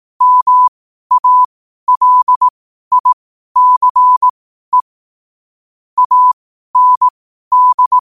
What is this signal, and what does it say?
Morse code, heard as a keyed tone, about 18 wpm, spaced as for 12 wpm, 1000 Hz. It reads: MALICE AND
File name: qotdfiles / 12wpm